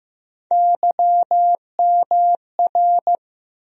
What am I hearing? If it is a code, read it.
Morse code, heard as a keyed tone, about 15 wpm, 700 Hz. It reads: YMR
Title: USA Call Sign Postfixes - 3 Letters 15wpm